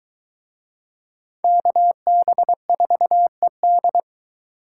Morse code, heard as a keyed tone, about 22 wpm, 700 Hz. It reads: KB4ED